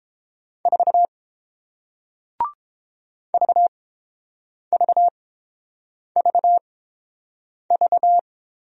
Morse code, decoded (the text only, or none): 4 VVVV